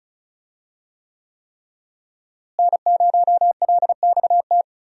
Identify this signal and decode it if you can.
Morse code, heard as a keyed tone, about 36 wpm, 700 Hz. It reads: N0LXT
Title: USA Call Signs - 5 Characters - Repeated 3x - 35wpm